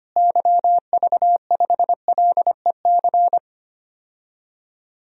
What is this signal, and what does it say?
Morse code, heard as a keyed tone, about 25 wpm, 700 Hz. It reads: YV5LEC
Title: DXCC-100 Call Signs - 6 Characters - Repeated 3x - 25wpm